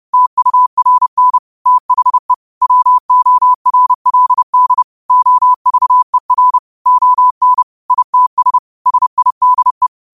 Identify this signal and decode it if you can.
Morse code, heard as a keyed tone, about 30 wpm, 1000 Hz. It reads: TURN THE WORLD OVER ON ITS SIDE
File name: qotdfiles / 30wpm